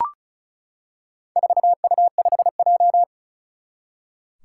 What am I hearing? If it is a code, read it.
Morse code, heard as a keyed tone, about 34 wpm, 700 Hz. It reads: 4U5J
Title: Four Challenging Letter-Number Combinations 35wpm